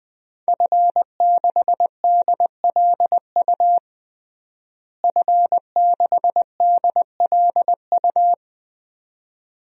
Morse code, heard as a keyed tone, about 21 wpm, 700 Hz. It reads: F6DLU F6DLU
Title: DXCC-100 Call Signs - 5 Characters - Repeated 2x - 20wpm